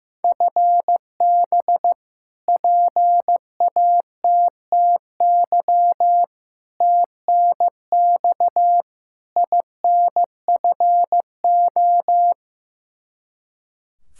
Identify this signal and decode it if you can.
Morse code, heard as a keyed tone, about 15 wpm, 700 Hz. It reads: FB PATTY TNX INFO
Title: K1USN SST QSO Practice 15wpm